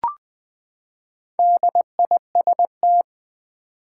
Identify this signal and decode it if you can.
Morse code, heard as a keyed tone, about 20 wpm, 700 Hz. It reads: DIST
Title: Build A Word - 4 Letters At A Time - 12-Letter Words - 20wpm